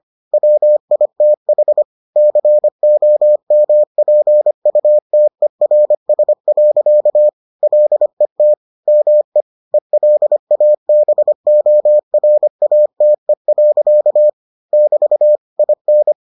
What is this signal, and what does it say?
Morse code, heard as a keyed tone, about 25 wpm, 600 Hz. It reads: WITH COMPUTERS. LET ME ELABORATE. <BT> IN